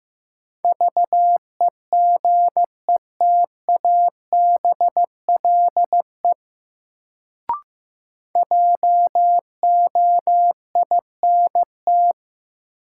Morse code, heard as a keyed tone, about 15 wpm, 700 Hz. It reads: VEGETABLE JOINT
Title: Sentences from Top 1300 Words - New Word Drill - 15wpm